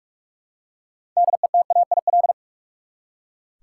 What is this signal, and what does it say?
Morse code, heard as a keyed tone, about 46 wpm, 700 Hz. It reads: DETAIL
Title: Six Letter Words - 45wpm